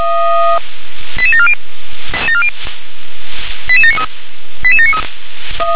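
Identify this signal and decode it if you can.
Five-tone selective call (ZVEI), heard as quick runs of five tones, tones of about 70 ms. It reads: 88740 88793 88793
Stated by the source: Five-tone ZVEI selective call received on an SDR.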